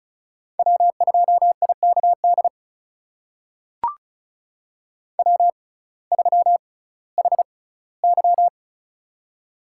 Morse code, heard as a keyed tone, about 36 wpm, spaced as for 19 wpm, 700 Hz. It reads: W2IKD W3HY
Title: USA Call Signs - 6x Character Spacing - 35wpm